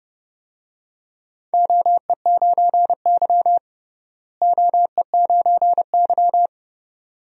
Morse code, heard as a keyed tone, about 32 wpm, 700 Hz. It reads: OE9Y OE9Y
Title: DXCC-100 Call Signs - 4 Characters - Repeated 2x - 30wpm